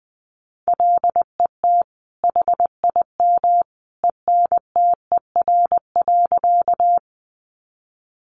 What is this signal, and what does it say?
Morse code, heard as a keyed tone, about 20 wpm, 700 Hz. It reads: LET HIM ENTER.
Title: Sentences from Top 1300 Words 20wpm